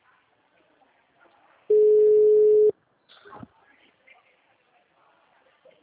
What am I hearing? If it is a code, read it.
Ringback tone (Europe, 425 Hz), with speech.
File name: out- -2002- - - .99352